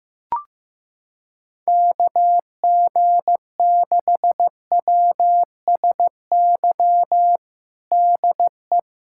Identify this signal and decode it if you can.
Morse code, heard as a keyed tone, about 15 wpm, 700 Hz. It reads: KG6WSY DE
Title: USA Callsign DE Callsign - 6 Characters - 15wpm